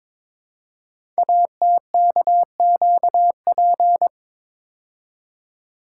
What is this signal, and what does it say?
Morse code, heard as a keyed tone, about 22 wpm, 700 Hz. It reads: ATKQP
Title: Code Groups - 5 Letters - 22wpm